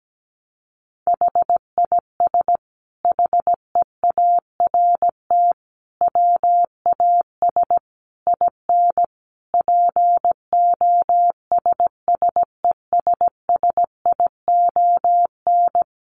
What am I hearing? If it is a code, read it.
Morse code, heard as a keyed tone, about 17 wpm, 700 Hz. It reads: HIS HEART WAS IN POSSESSION
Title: Sentences from Top 1400 Words 17wpm